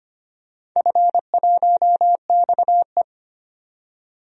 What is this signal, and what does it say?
Morse code, heard as a keyed tone, about 25 wpm, 700 Hz. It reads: F1XE
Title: DXCC-100 Call Signs - 4 Characters - 25wpm